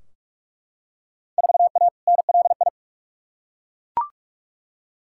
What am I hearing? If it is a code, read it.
Morse code, heard as a keyed tone, about 41 wpm, 700 Hz. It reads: 4A NLI
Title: Field Day Exchanges Encore - 45wpm